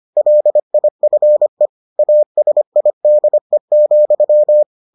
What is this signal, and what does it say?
Morse code, heard as a keyed tone, about 25 wpm, 600 Hz. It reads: LIFE ASIDE,